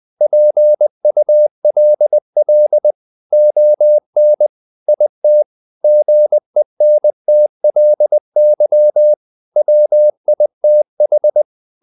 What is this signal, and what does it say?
Morse code, heard as a keyed tone, about 20 wpm, 600 Hz. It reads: PULL ON IT GENTLY WITH
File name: the old man and the sea part 2 20wpm